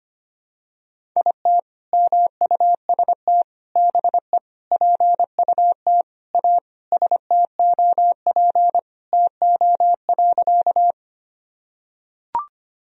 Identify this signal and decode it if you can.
Morse code, heard as a keyed tone, about 25 wpm, 700 Hz. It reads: IT MUST BE PUT A STOP TO.
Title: Sentences from Top 1000 Words - Review Part 1 - 25wpm